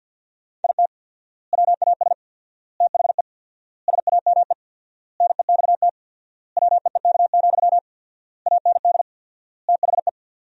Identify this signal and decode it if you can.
Morse code, heard as a keyed tone, about 50 wpm, 700 Hz. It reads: IT WAS THE SAME NEXT WEEK, AND THE
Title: 4x Word-Spacing - Sentences from Top 500 Words - 50wpm